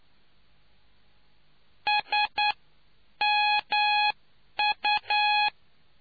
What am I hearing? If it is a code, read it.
Morse code, heard as a keyed tone, about 9 wpm, spaced as for 8 wpm, 840 Hz. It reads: SMU